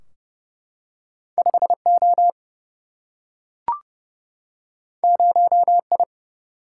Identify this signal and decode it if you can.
Morse code, heard as a keyed tone, about 30 wpm, 700 Hz. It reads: 5 O 0 I
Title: Single-Letter Single-Number 30wpm